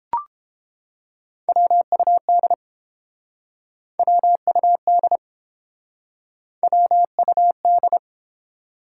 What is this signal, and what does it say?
Morse code, heard as a keyed tone, about 30 wpm, 700 Hz. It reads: WUD WUD WUD